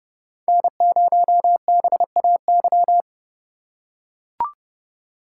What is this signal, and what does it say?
Morse code, heard as a keyed tone, about 29 wpm, 700 Hz. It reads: N0BAY
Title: USA Call Signs - 5 Characters - 2x Character Spacing - 30wpm